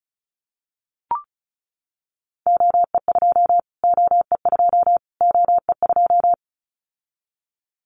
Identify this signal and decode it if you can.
Morse code, heard as a keyed tone, about 34 wpm, 700 Hz. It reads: OE2 OE2 OE2
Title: Callsign Prefixes for Top 25 DXCC - Repeated 3x - 35wpm